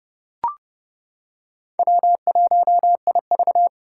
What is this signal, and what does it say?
Morse code, heard as a keyed tone, about 30 wpm, 700 Hz. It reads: W1IV
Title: Four Challenging Letter-Number Combinations 30wpm